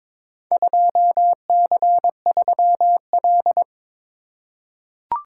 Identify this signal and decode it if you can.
Morse code, heard as a keyed tone, about 22 wpm, 700 Hz. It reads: 2C3L
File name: practice / 22wpm / Four Letter-Number Combinations - 22wpm